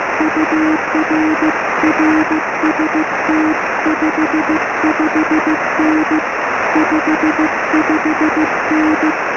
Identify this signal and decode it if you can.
Morse code, heard as a keyed tone, about 15 wpm, 310 Hz, with noise. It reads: UR RST 55N 55N
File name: sm7lcb / g4cch 23cm